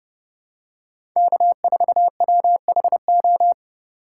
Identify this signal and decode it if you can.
Morse code, heard as a keyed tone, about 30 wpm, 700 Hz. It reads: K4WHO
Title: USA Call Signs - 5 Characters - 4x Character Spacing - 30wpm